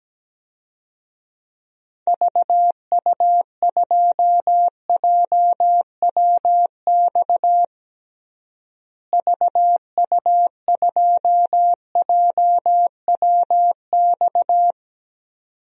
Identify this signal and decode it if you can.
Morse code, heard as a keyed tone, about 17 wpm, 700 Hz. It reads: VU2JWX VU2JWX
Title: DXCC-100 Call Signs - 6 Characters - Repeated 2x - 17wpm